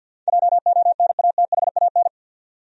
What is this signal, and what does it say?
Morse code, heard as a keyed tone, about 52 wpm, 700 Hz. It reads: JONATHAN
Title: USA Common Names - 50wpm